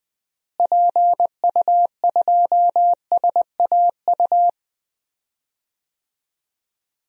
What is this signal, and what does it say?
Morse code, heard as a keyed tone, about 20 wpm, 700 Hz. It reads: PU2SAU
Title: DXCC-100 Call Signs - 6 Characters - Repeated 2x - 20wpm